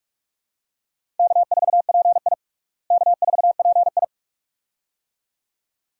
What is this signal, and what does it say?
Morse code, heard as a keyed tone, about 46 wpm, 700 Hz. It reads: K4PI K4PI
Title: USA Call Signs - 4 Characters - Repeated 2x - 45wpm